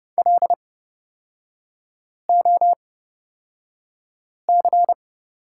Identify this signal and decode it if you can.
Morse code, heard as a keyed tone, about 30 wpm, spaced as for 5 wpm, 700 Hz. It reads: LOC
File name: Lesson 36e - BT - Three Characters 30wpm